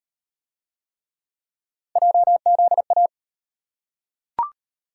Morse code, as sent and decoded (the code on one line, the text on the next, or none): .--- --.. .-
JZA